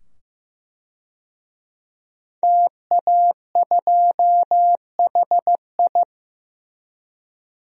Morse code, heard as a keyed tone, about 16 wpm, 700 Hz. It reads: TA2HI
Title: DXCC-100 Call Signs - 5 Characters - 15wpm